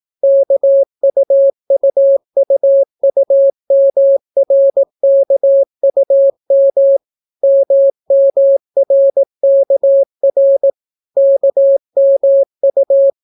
Morse code, heard as a keyed tone, about 18 wpm, 550 Hz. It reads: KUUUUMRKUM MMRKR KMU